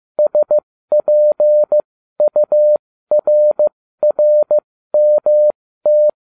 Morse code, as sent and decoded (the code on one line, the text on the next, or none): ... .--. ..- .-. .-. -- -
SPURRMT